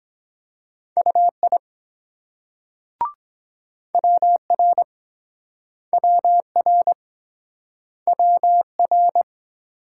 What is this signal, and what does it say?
Morse code, heard as a keyed tone, about 23 wpm, 700 Hz. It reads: UI WR WR WR